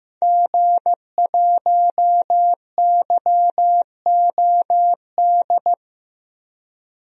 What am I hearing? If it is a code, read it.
Morse code, heard as a keyed tone, about 15 wpm, 700 Hz. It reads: G1YOD
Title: DXCC-100 Call Signs - 5 Characters - 15wpm